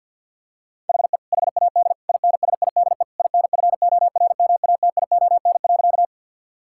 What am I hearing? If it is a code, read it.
Morse code, heard as a keyed tone, about 50 wpm, 700 Hz. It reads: HE HAD INSIDE INFORMATION.
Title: Sentences from Top 1000 Words - Review Part 6 - 50wpm